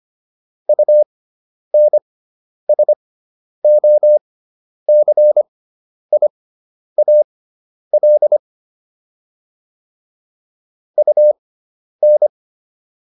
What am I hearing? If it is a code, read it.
Morse code, heard as a keyed tone, about 25 wpm, spaced as for 10 wpm, 600 Hz. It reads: UNSOCIAL UN